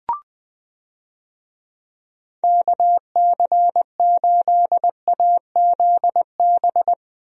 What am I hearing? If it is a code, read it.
Morse code, heard as a keyed tone, about 20 wpm, 700 Hz. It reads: KC8AZB
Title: USA Call Signs - 6 Characters - 20wpm